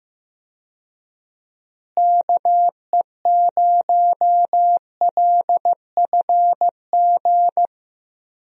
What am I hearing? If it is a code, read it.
Morse code, heard as a keyed tone, about 15 wpm, 700 Hz. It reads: KE0LFG
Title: USA Call Signs - 6 Characters - Repeated 2x - 15wpm